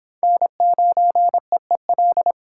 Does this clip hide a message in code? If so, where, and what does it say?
Yes: Morse code from 0.2 to 2.3 s, heard as a keyed tone, about 22 wpm, 700 Hz. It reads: N9EEL